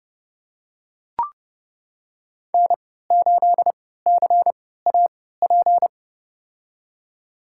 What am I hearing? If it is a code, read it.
Morse code, heard as a keyed tone, about 30 wpm, spaced as for 17 wpm, 700 Hz. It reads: N8CAP
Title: USA Call Signs - 3x Character Spacing - 30wpm